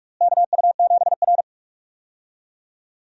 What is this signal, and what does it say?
Morse code, heard as a keyed tone, about 46 wpm, 700 Hz. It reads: KU7R
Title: USA Call Signs - 4 Characters - Repeated 2x - 45wpm